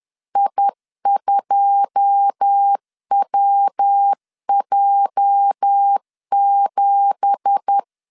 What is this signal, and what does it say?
Morse code, heard as a keyed tone, about 10 wpm, 800 Hz. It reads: I2WJ7
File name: LEZIONI / 9 2867 10wpm 2733